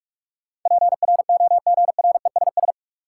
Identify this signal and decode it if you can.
Morse code, heard as a keyed tone, about 41 wpm, 700 Hz. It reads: PROGRESS
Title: Sentences from Top 1400 Words - New Word Drill - 45wpm